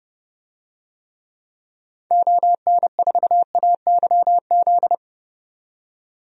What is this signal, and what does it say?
Morse code, heard as a keyed tone, about 30 wpm, 700 Hz. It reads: ON4AYZ
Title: DXCC-100 Call Signs - 6 Characters - 30wpm